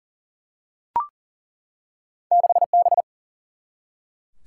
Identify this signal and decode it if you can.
Morse code, heard as a keyed tone, about 38 wpm, 700 Hz. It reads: 6B
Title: Single-Letter Single-Number 40wpm